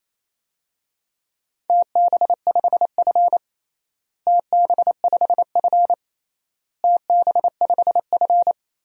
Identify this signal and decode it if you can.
Morse code, heard as a keyed tone, about 28 wpm, 700 Hz. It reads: TB5F TB5F TB5F
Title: DXCC-100 Call Signs - 4 Characters - Repeated 3x - 28wpm